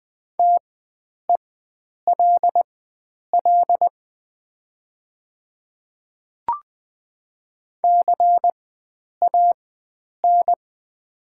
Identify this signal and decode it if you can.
Morse code, heard as a keyed tone, about 20 wpm, spaced as for 9 wpm, 700 Hz. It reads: TELL CAN
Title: 4x Character Spacing - Top 500 Words - 20wpm